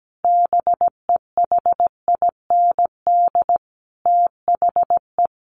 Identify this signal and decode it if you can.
Morse code, heard as a keyed tone, about 17 wpm, 700 Hz. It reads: BEHIND THE
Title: A Sound Of Thunder 17wpm